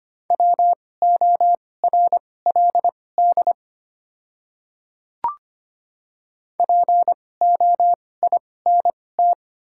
Morse code, heard as a keyed tone, about 25 wpm, spaced as for 18 wpm, 690 Hz. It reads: WORLD POINT
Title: 2x Character Spacing - Top 300 Words - 25wpm